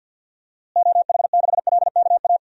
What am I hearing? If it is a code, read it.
Morse code, heard as a keyed tone, about 51 wpm, 700 Hz. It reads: OH6LKA
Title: DXCC-100 Call Signs - 6 Characters - Repeated 3x - 50wpm